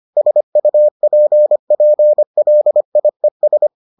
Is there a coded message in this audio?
Yes: Morse code from 0.2 to 3.7 s, heard as a keyed tone, about 25 wpm, 600 Hz. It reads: SUPPLIES